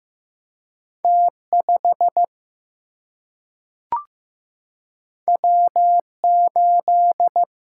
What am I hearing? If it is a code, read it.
Morse code, heard as a keyed tone, about 15 wpm, 700 Hz. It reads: T5 W8
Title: Single-Letter Single-Number 15wpm